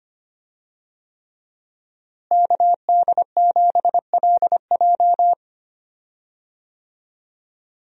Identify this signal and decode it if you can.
Morse code, heard as a keyed tone, about 25 wpm, 700 Hz. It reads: KD7LJ
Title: USA Call Signs - 5 Characters - Repeated 3x - 25wpm